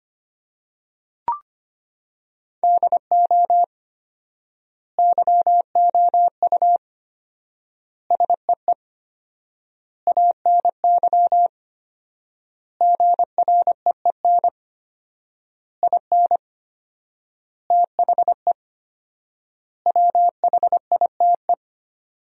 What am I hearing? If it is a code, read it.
Morse code, heard as a keyed tone, about 25 wpm, 700 Hz. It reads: DO YOU SEE ANY GREEN IN THE WHITE
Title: 4x Word-Spacing - Sentences from Top 500 Words - 25wpm